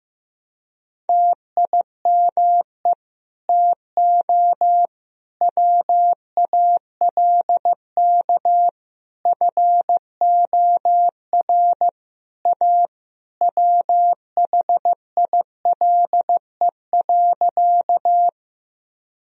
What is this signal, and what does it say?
Morse code, heard as a keyed tone, about 15 wpm, 700 Hz. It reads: TIME TO WALK FOR A WHILE.
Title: Sentences from Top 1000 Words - Review Part 1 - 15wpm